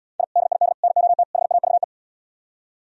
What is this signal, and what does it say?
Morse code, heard as a keyed tone, about 77 wpm, 700 Hz. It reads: I DID NOT HAVE